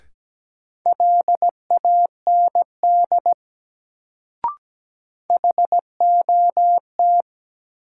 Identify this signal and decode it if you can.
Morse code, heard as a keyed tone, about 17 wpm, 700 Hz. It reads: LAND HOT
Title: Sentences from Top 500 Words - New Word Drill - 17wpm